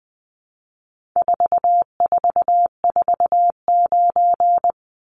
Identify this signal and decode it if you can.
Morse code, heard as a keyed tone, about 20 wpm, 700 Hz. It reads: 4449
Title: Numbers - 4-Digit 20wpm